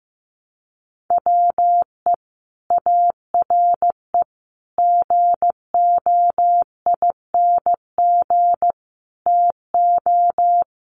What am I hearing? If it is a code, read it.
Morse code, heard as a keyed tone, about 15 wpm, 700 Hz. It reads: WE ARE GOING TO
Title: Sets of 5 Words Encore 15wpm